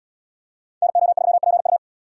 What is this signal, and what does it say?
Morse code, heard as a keyed tone, about 70 wpm, 700 Hz. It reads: NO3PU